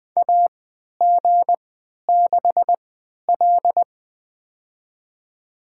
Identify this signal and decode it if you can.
Morse code, heard as a keyed tone, about 21 wpm, spaced as for 11 wpm, 700 Hz. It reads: AG6L
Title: USA Call Signs - 4 Characters - 3x Character Spacing - 20wpm